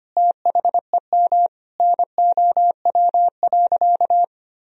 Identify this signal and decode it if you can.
Morse code, heard as a keyed tone, about 25 wpm, 700 Hz. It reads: THEM NOW.
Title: Sentences from Top 1000 Words - Review Part 1 - 25wpm